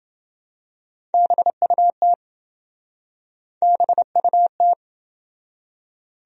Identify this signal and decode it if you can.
Morse code, heard as a keyed tone, about 27 wpm, 700 Hz. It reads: BUT BUT